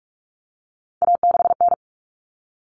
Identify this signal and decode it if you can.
Morse code, heard as a keyed tone, about 44 wpm, 700 Hz. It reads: A6N
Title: All Cut Numbers - 3-Digit 45wpm